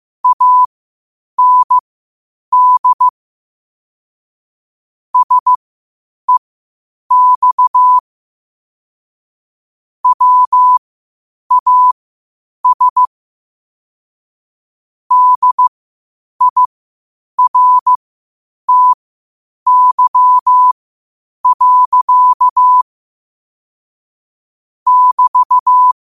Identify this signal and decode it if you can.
Morse code, heard as a keyed tone, about 15 wpm, spaced as for 8 wpm, 1000 Hz. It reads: AND SEX WAS DIRTY. <BT>
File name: qotdfiles / 8wpm